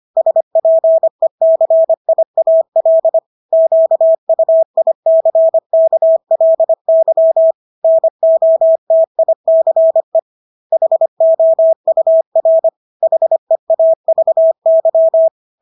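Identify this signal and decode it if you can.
Morse code, heard as a keyed tone, about 25 wpm, 650 Hz. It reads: SPECIAL QUICKLY NOTICE HOUR HEAVY
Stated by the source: Morse practice of English words at 25 wpm from CW med Gnister.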